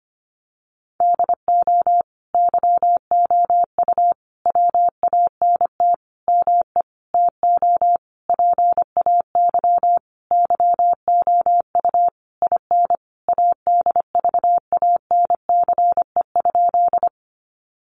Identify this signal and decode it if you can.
Morse code, heard as a keyed tone, about 25 wpm, 700 Hz. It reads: DO YOU WANT ME TO PAY YOU IN ADVANCE?
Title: Sentences from Top 1400 Words 25wpm